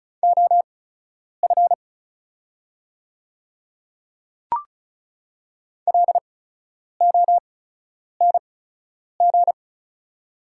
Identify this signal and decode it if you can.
Morse code, heard as a keyed tone, about 36 wpm, spaced as for 10 wpm, 700 Hz. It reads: OF LONG
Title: 8x Character Spacing - Top 200 Words - 35wpm